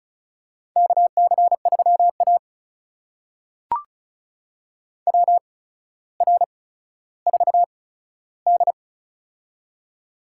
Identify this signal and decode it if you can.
Morse code, heard as a keyed tone, about 35 wpm, spaced as for 15 wpm, 700 Hz. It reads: KC3A WR4D
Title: USA Call Signs - 4 Characters - 8x Character Spacing - 35wpm